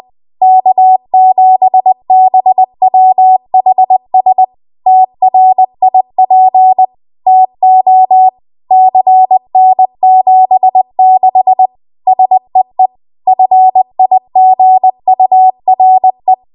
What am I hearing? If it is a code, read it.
Morse code, heard as a keyed tone, about 20 wpm, 750 Hz. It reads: K7BWHS TRIP TO CN76 SEE FIGURE
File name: Morse / 20WPM